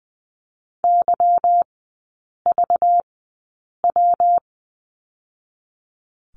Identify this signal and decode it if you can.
Morse code, heard as a keyed tone, about 20 wpm, spaced as for 8 wpm, 700 Hz. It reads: YVW
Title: 218 Lesson 35e - Number 0 - Three Characters 20wpm